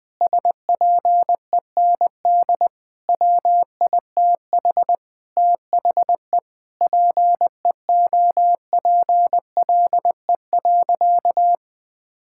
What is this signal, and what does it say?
Morse code, heard as a keyed tone, about 20 wpm, 700 Hz. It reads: SPEND WITH THE PEOPLE.
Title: Sentences from Top 1000 Words - Review Part 5 - 20wpm